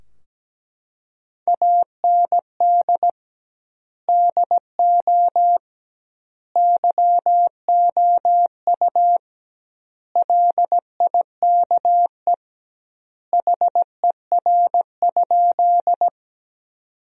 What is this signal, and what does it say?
Morse code, heard as a keyed tone, about 17 wpm, 700 Hz. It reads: AND DO YOU LIKE HER?
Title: 2x Word-Spacing - Sentences from Top 100 Words - 17wpm